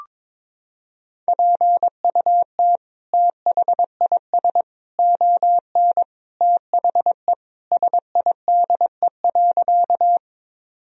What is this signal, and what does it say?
Morse code, heard as a keyed tone, about 22 wpm, 700 Hz. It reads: PUT THIS ON THE SIDE.
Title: Sentences from Top 1000 Words - Review Part 1 - 22wpm